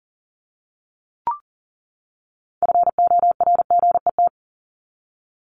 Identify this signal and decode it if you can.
Morse code, heard as a keyed tone, about 40 wpm, 700 Hz. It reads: FORGET
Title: Top 1001-1250 Words 40wpm